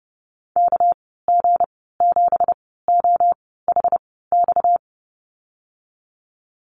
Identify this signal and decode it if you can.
Morse code, heard as a keyed tone, about 30 wpm, spaced as for 17 wpm, 700 Hz. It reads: KG7OHX
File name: Lesson 37g - Letter x - Call Signs 30wpm